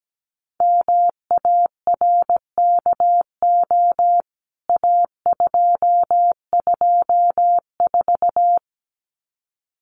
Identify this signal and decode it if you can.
Morse code, heard as a keyed tone, about 17 wpm, 700 Hz. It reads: MARKO A224
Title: CWT Exchange 17wpm